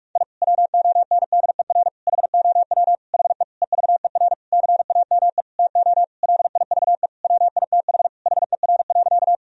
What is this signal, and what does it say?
Morse code, heard as a keyed tone, about 41 wpm, 700 Hz. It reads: I WONDER HOW HE EVER CAME TO LIVE WITH HER.
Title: Sentences from Top 1000 Words - Review Part 3 - 45wpm